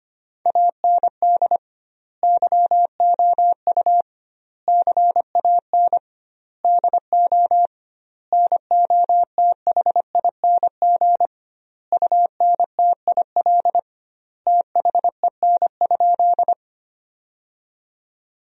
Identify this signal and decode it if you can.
Morse code, heard as a keyed tone, about 25 wpm, 700 Hz. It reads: AND YOU CAN DO NOTHING UNTIL THEN?
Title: 2x Word-Spacing - Sentences from Top 500 Words - 25wpm